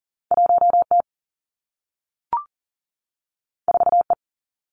Morse code, heard as a keyed tone, about 42 wpm, 700 Hz. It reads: 1T 4E
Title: Two Challenging Letter-Number Combinations 40wpm